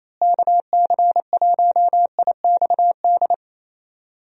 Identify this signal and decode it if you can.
Morse code, heard as a keyed tone, about 27 wpm, 700 Hz. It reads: KC1IXD
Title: USA Call Signs - 6 Characters - Repeated 3x - 28wpm